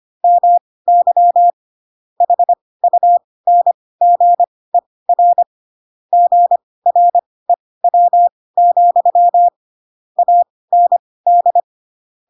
Morse code, heard as a keyed tone, about 25 wpm, spaced as for 18 wpm, 700 Hz. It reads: MY HUNGER GREW, AND
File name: Pinocchio / Pinocchio ch7 12 25 ik1hge